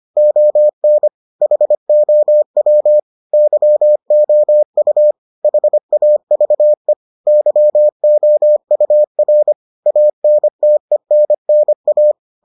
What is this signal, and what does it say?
Morse code, heard as a keyed tone, about 25 wpm, 600 Hz. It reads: ON HOW YOU HAVE YOUR ANTENNA